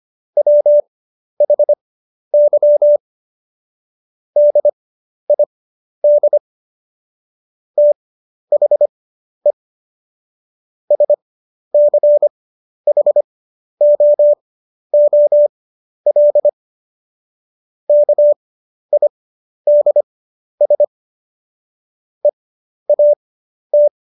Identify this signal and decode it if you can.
Morse code, heard as a keyed tone, about 25 wpm, spaced as for 11 wpm, 590 Hz. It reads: WHY DID THE SCHOOL KIDS EAT
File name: Jokes-25-char-10-eff